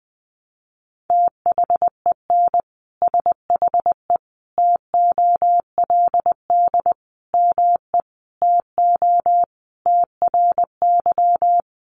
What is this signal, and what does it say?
Morse code, heard as a keyed tone, about 20 wpm, 700 Hz. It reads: THEN SHE TOLD ME TO TRY
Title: Sentences from Top 400 Words 20wpm